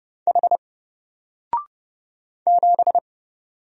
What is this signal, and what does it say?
Morse code, heard as a keyed tone, about 30 wpm, 700 Hz. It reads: H 7
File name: Single-Letter-Number-30wpm